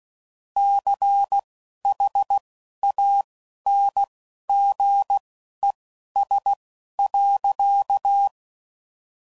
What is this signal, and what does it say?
Morse code, heard as a keyed tone, about 16 wpm, spaced as for 11 wpm, 800 Hz. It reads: CHANGES.
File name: CW Code Groups @10WPM / Technical Subjects / XC